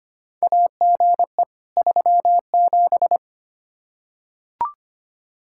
Morse code, heard as a keyed tone, about 25 wpm, 700 Hz. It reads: AGE 37
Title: K1USN SST QSO Practice 25wpm